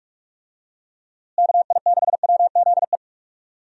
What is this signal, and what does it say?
Morse code, heard as a keyed tone, about 46 wpm, 700 Hz. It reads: KI6WZE
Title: USA Call Signs - 6 Characters - Repeated 2x - 45wpm